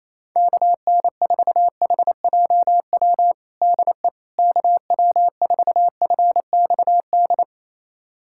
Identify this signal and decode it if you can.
Morse code, heard as a keyed tone, about 28 wpm, 700 Hz. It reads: KN4HJW DE KW4FXD
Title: USA Callsign DE Callsign - 6 Characters - 28wpm